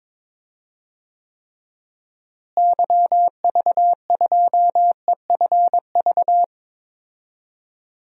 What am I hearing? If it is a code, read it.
Morse code, heard as a keyed tone, about 22 wpm, 700 Hz. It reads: YV2EFV
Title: DXCC-100 Call Signs - 6 Characters - Repeated 2x - 22wpm